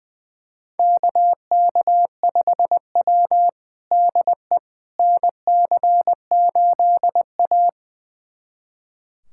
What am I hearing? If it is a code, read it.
Morse code, heard as a keyed tone, about 20 wpm, 700 Hz. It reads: KK5W DE NC8A
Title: USA Callsign DE Callsign - 4 Characters - 20wpm